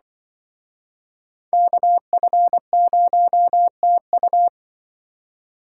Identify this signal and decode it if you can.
Morse code, heard as a keyed tone, about 24 wpm, 700 Hz. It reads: KF0TU
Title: Speed Racing - USA 2x2 Call Signs 20wpm